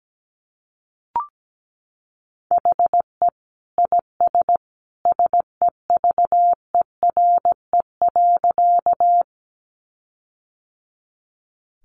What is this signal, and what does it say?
Morse code, heard as a keyed tone, about 17 wpm, 700 Hz. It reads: HE IS SEVERE.
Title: Sentences from Top 1400 Words 17wpm